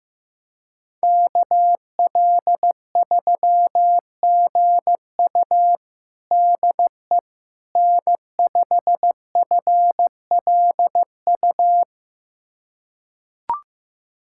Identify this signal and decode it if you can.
Morse code, heard as a keyed tone, about 15 wpm, 700 Hz. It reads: KL3GU DE N5FLU
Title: USA Callsign DE Callsign - 5 Characters - 15wpm